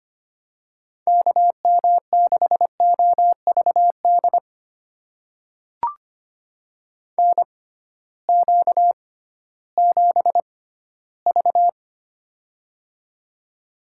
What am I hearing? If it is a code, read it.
Morse code, heard as a keyed tone, about 25 wpm, spaced as for 15 wpm, 700 Hz. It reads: KM6OVD NQ7V
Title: USA Call Signs - 6x Character Spacing - 25wpm